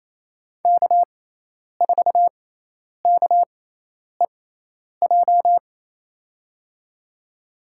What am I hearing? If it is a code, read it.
Morse code, heard as a keyed tone, about 28 wpm, spaced as for 10 wpm, 700 Hz. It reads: K4KEJ
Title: USA Call Signs - 6x Character Spacing - 28wpm